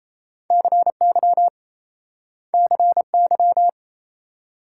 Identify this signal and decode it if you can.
Morse code, heard as a keyed tone, about 38 wpm, 700 Hz. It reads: CY CY